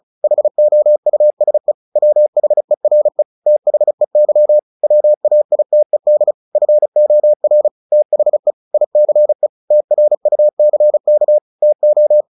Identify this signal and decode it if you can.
Morse code, heard as a keyed tone, about 34 wpm, 600 Hz. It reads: HOUSE WHERE THEY WAITED FOR THE ICE TRUCK TO
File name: the old man and the sea / the old man and the sea 1 35wpm